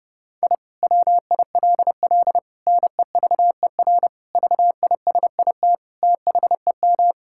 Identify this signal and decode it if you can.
Morse code, heard as a keyed tone, about 30 wpm, 700 Hz. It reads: I WILL NEVER VISIT THEM
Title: Sentences from Top 1000 Words - Review Part 6 - 30wpm